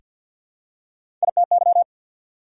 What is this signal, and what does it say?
Morse code, heard as a keyed tone, about 48 wpm, 690 Hz. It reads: ITY
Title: Prefixes and Suffixes 50wpm